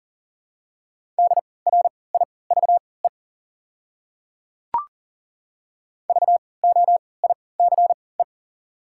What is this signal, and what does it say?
Morse code, heard as a keyed tone, about 40 wpm, spaced as for 23 wpm, 700 Hz. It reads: DRIVE VOICE
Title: 3x Character Spacing - Top 500 Words - 40wpm